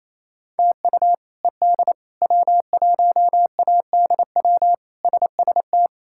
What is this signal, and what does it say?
Morse code, heard as a keyed tone, about 27 wpm, 700 Hz. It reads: TU ED W1ADW SST
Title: K1USN SST QSO Practice - 28wpm